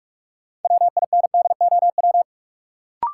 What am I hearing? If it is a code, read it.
Morse code, heard as a keyed tone, about 46 wpm, 700 Hz. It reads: WINDOW
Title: Sentences from Top 1600 Words - New Word Drill - 45wpm